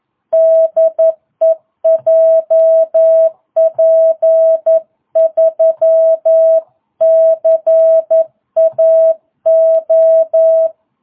Morse code, heard as a keyed tone, about 11 wpm, 660 Hz. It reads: DE JP3CAO